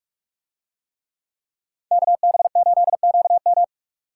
Morse code, heard as a keyed tone, about 45 wpm, 700 Hz. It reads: KB8QM